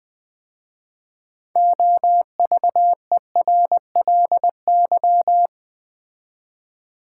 Morse code, heard as a keyed tone, about 20 wpm, 700 Hz. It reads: OVERLY
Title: 6 Letter Words - First 3 Letters Given - 20wpm